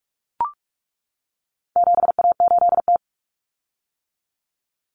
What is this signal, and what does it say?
Morse code, heard as a keyed tone, about 44 wpm, 700 Hz. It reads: 7A8T
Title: Common Cut Numbers - 4-Digit 45wpm